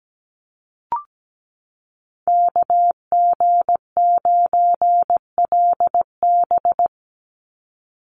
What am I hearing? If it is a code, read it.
Morse code, heard as a keyed tone, about 17 wpm, 700 Hz. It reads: KG9LB
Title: Call Signs Repeated 2x - 17wpm